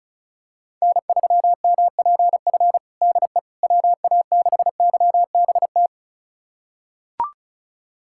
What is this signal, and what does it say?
Morse code, heard as a keyed tone, about 34 wpm, 700 Hz. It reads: N3MPF DE WA6YBT
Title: USA Callsign DE Callsign 35wpm